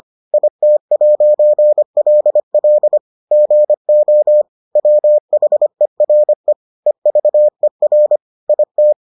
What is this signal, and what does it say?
Morse code, heard as a keyed tone, about 25 wpm, 600 Hz. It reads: IT'LL GO WHERE EVER IT